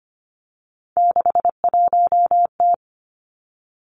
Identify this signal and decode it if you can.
Morse code, heard as a keyed tone, about 25 wpm, 700 Hz. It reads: 61T
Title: Three Challenging Letter-Number Combinations 25wpm